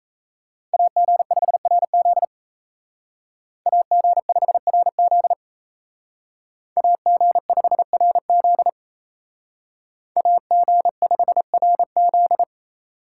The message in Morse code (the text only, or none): AG5RZ AG5RZ AG5RZ AG5RZ